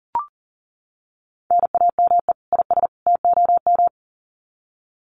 Morse code, heard as a keyed tone, about 40 wpm, 700 Hz. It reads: NAME IS TOM
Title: Common QSO Phrases 40wpm